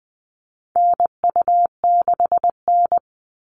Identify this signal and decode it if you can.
Morse code, heard as a keyed tone, about 20 wpm, 700 Hz. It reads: NU6N
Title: Call Signs Repeated 3x - 20wpm